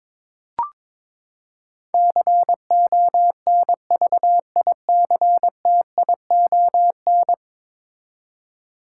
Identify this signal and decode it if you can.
Morse code, heard as a keyed tone, about 22 wpm, 700 Hz. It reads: CONVICTION
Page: morse-code-ninja-practice-22wpm